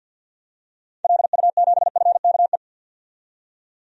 Morse code, heard as a keyed tone, about 51 wpm, 700 Hz. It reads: LU7FKE
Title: DXCC-100 Call Signs - 6 Characters - 50wpm